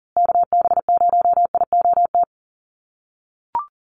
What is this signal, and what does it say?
Morse code, heard as a keyed tone, about 40 wpm, 700 Hz. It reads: KB0IOT
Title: USA Call Signs Repeated 2x 40wpm